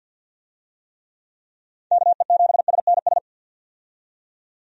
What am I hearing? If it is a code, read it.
Morse code, heard as a keyed tone, about 46 wpm, 700 Hz. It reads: KE7SNS
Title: USA Call Signs - 6 Characters - 50wpm